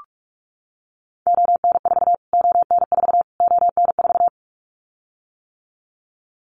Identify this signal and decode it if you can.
Morse code, heard as a keyed tone, about 44 wpm, 700 Hz. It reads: ON4 ON4 ON4
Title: Callsign Prefixes for Top 25 DXCC - Repeated 3x - 45wpm